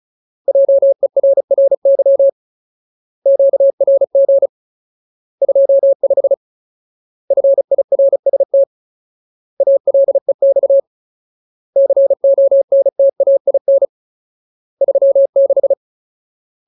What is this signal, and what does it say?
Morse code, heard as a keyed tone, about 34 wpm, 550 Hz. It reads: JERRY QRG 25 FIRST ALEX CONTAIN 36
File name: Ultimate Word List / 35WPM Ultimate